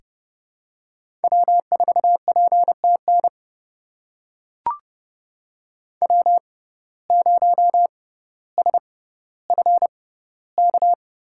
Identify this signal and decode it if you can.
Morse code, heard as a keyed tone, about 30 wpm, spaced as for 15 wpm, 690 Hz. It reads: W4PTN W0SFK
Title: USA Call Signs - 5 Characters - 6x Character Spacing - 30wpm